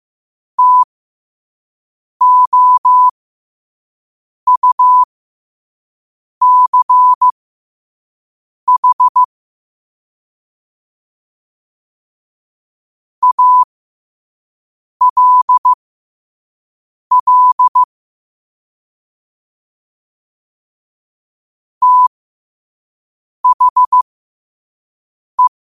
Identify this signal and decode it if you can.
Morse code, heard as a keyed tone, about 15 wpm, spaced as for 5 wpm, 1000 Hz. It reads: TOUCH ALL THE
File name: qotdfiles / 5wpm